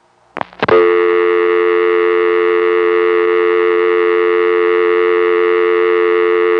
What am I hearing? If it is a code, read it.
Dial tone (North America), 350 + 440 Hz.